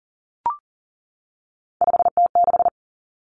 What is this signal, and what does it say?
Morse code, heard as a keyed tone, about 40 wpm, 700 Hz. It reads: 5T6
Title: Common Cut Numbers - 3-Digit 40wpm